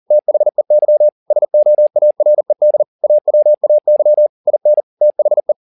Morse code, heard as a keyed tone, about 40 wpm, 600 Hz. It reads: THEY SOARED AWAY IN THE